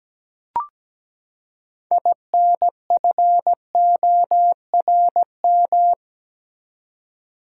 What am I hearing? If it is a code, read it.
Morse code, heard as a keyed tone, about 17 wpm, 700 Hz. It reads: INFORM
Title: Six Letter Words - 17wpm